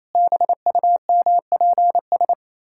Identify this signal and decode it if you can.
Morse code, heard as a keyed tone, about 27 wpm, 700 Hz. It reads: BUMPS
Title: Pangram Challenge v2 - 28wpm